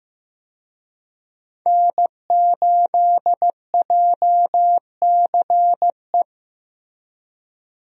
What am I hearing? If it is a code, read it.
Morse code, heard as a keyed tone, about 15 wpm, 700 Hz. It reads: N8JCE
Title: USA Call Signs - 5 Characters - Repeated 3x - 15wpm